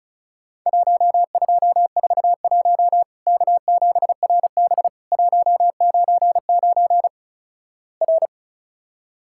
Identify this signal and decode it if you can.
Morse code, heard as a keyed tone, about 36 wpm, 700 Hz. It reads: 1241 K7RB 199 R
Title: Worked All Europe DX Contest - QTC Exchange - 35wpm